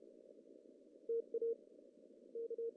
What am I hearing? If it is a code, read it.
Morse code, heard as a keyed tone, about 32 wpm, 460 Hz. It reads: TA K
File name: CW / cqww-cw-k5tr